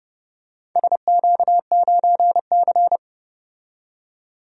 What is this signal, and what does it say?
Morse code, heard as a keyed tone, about 31 wpm, 700 Hz. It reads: SQ9C
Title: DXCC-100 Call Signs - 4 Characters - 30wpm